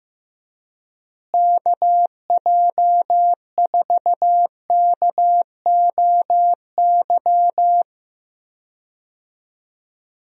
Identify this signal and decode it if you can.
Morse code, heard as a keyed tone, about 15 wpm, 700 Hz. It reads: KJ4KOY
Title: USA Call Signs - 6 Characters - Repeated 3x - 15wpm